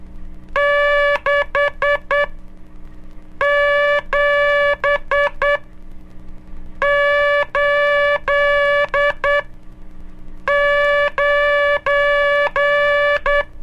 Morse code, heard as a keyed tone, about 10 wpm, spaced as for 5 wpm, 580 Hz, with noise. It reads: 6789
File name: cursos / Curso CW mp3 / CW Aula - 11